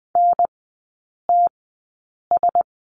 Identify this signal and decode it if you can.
Morse code, heard as a keyed tone, about 20 wpm, spaced as for 8 wpm, 690 Hz. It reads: NTS
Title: 093 Lesson 16e - Letter c - Three Characters 20wpm